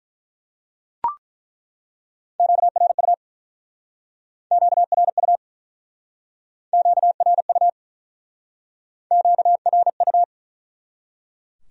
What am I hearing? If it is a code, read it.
Morse code, heard as a keyed tone, about 49 wpm, 700 Hz. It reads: QRU QRU QRU QRU